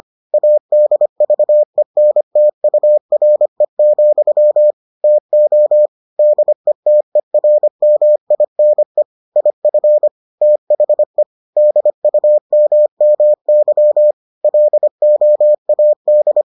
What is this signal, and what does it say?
Morse code, heard as a keyed tone, about 25 wpm, 600 Hz. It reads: ADVENTURE, TO DETERMINE IF THE DUMMY LOAD